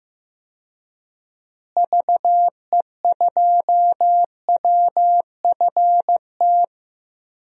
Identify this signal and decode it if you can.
Morse code, heard as a keyed tone, about 15 wpm, 700 Hz. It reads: VE2WFT
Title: DXCC-100 Call Signs - 6 Characters - Repeated 3x - 15wpm